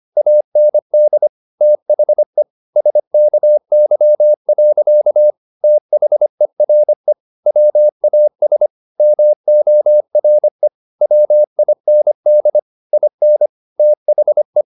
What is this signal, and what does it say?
Morse code, heard as a keyed tone, about 25 wpm, 600 Hz. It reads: AND THE SKY. THERE WAS MORE WIND IN THE